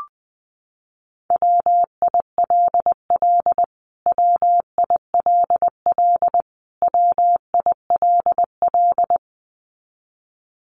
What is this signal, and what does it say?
Morse code, heard as a keyed tone, about 20 wpm, 700 Hz. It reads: WILL WILL WILL
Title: Top 300 Words Repeated 20wpm